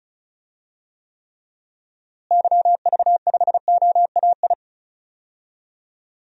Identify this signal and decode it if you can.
Morse code, heard as a keyed tone, about 36 wpm, 700 Hz. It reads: YV5OAI
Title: DXCC-100 Call Signs - 6 Characters - Repeated 3x - 35wpm